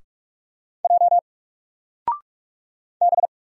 Morse code, heard as a keyed tone, about 44 wpm, 700 Hz. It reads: J B
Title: Single Character (Letter:Number:Punctuation:Prosign) 45wpm